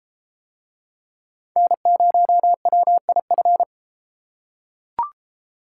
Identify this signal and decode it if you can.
Morse code, heard as a keyed tone, about 32 wpm, 700 Hz. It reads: N0WIF